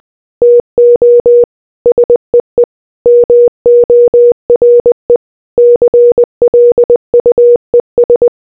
Morse code, heard as a keyed tone, about 20 wpm, 480 Hz. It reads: TO SEE MORE CLUES